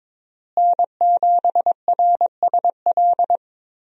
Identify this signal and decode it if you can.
Morse code, heard as a keyed tone, about 23 wpm, 700 Hz. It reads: N7RSL
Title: USA Call Signs - 5 Characters - Repeated 3x - 22wpm